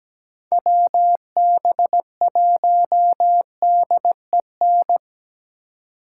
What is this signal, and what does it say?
Morse code, heard as a keyed tone, about 20 wpm, 700 Hz. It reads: WB1DEN
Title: Speed Racing - USA Call Signs 17wpm